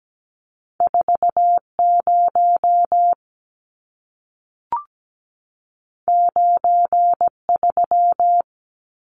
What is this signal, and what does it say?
Morse code, heard as a keyed tone, about 17 wpm, 700 Hz. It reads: 40 93
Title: Numbers - 2-Digit 17wpm